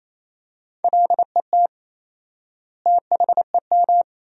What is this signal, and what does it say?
Morse code, heard as a keyed tone, about 28 wpm, 700 Hz. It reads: LET THEM
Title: 4x Word-Spacing - Sentences from Top 300 Words - 28wpm